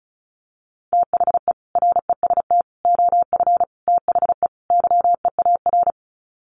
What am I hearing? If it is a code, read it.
Morse code, heard as a keyed tone, about 34 wpm, 700 Hz. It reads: THE REST OF THE YEAR
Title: Sets of 5 Words Encore 35wpm